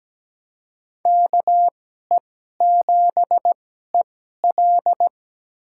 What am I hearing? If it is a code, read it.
Morse code, heard as a keyed tone, about 17 wpm, spaced as for 12 wpm, 700 Hz. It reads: KE7EL
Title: USA Call Signs - 5 Characters - 2x Character Spacing - 17wpm